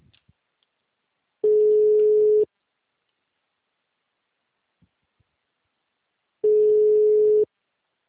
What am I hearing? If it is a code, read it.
Ringback tone (Europe, 425 Hz).